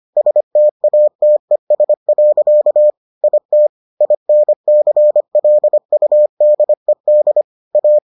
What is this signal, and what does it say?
Morse code, heard as a keyed tone, about 25 wpm, 600 Hz. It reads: STATES. IT INCLUDED A